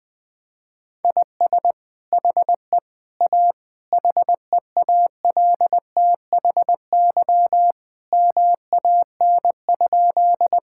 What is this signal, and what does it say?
Morse code, heard as a keyed tone, about 20 wpm, 700 Hz. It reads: IS HE A HEALTHY MAN?
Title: Sentences from Top 1000 Words - Review Part 5 - 20wpm